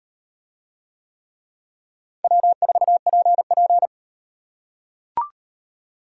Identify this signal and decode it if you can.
Morse code, heard as a keyed tone, about 36 wpm, 700 Hz. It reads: W4PP